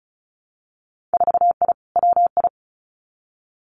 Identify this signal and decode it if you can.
Morse code, heard as a keyed tone, about 36 wpm, 700 Hz. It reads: 4I WI
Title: Winter Field Day Exchanges 35wpm